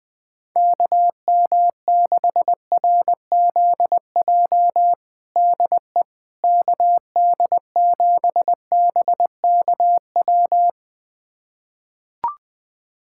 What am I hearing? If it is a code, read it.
Morse code, heard as a keyed tone, about 20 wpm, 700 Hz. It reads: KM6RZJ DE KD7BKW
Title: USA Callsign DE Callsign - 6 Characters - 20wpm